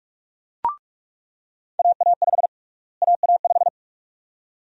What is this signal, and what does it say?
Morse code, heard as a keyed tone, about 46 wpm, 700 Hz. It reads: AA5 AA5
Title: USA Call Sign Prefixes - Repeated 2x - 45wpm